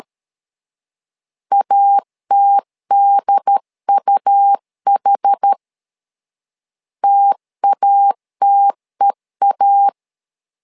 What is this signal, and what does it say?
Morse code, heard as a keyed tone, about 13 wpm, 800 Hz. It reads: ATDUH TATEA
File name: LEZIONI / 2 UHDN / 2 UHD 12wpm 1540